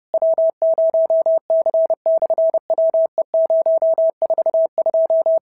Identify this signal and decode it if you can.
Morse code, heard as a keyed tone, about 30 wpm, 650 Hz. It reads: W0C/WE042
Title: SOTA QSO Practice 30wpm